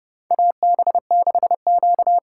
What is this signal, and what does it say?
Morse code, heard as a keyed tone, about 30 wpm, 700 Hz. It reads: AB6Q
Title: USA Call Signs - 4 Characters - 4x Character Spacing - 30wpm